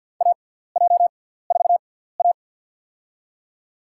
Morse code, heard as a keyed tone, about 49 wpm, spaced as for 17 wpm, 700 Hz. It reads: AJ4A
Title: USA Call Signs - 6x Character Spacing - 50wpm